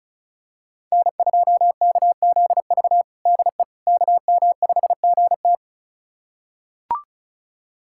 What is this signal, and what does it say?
Morse code, heard as a keyed tone, about 36 wpm, 700 Hz. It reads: N2KZV DE KM5GT
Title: USA Callsign DE Callsign - 5 Characters - 35wpm